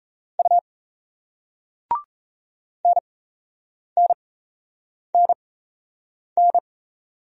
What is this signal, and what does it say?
Morse code, heard as a keyed tone, about 35 wpm, spaced as for 8 wpm, 700 Hz. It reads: U NNNN
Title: Speed-Racing Single Letters 28wpm